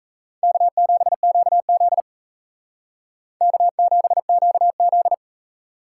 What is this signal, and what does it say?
Morse code, heard as a keyed tone, about 37 wpm, 700 Hz. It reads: K7QZ K7QZ